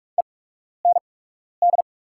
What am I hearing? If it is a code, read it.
Morse code, heard as a keyed tone, about 48 wpm, spaced as for 12 wpm, 700 Hz. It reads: END
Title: 8x Character Spacing - Top 500 Words - 45wpm